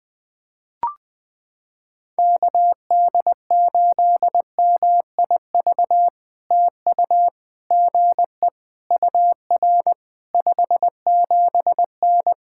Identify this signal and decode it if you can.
Morse code, heard as a keyed tone, about 20 wpm, 700 Hz. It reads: KD8MIV TU GE UR 57N
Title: POTA QSO Practice - 20wpm